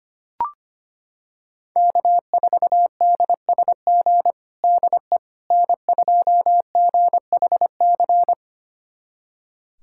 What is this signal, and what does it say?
Morse code, heard as a keyed tone, about 25 wpm, 700 Hz. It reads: K4DSG DE N2GHC
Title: USA Callsign DE Callsign - 5 Characters - 25wpm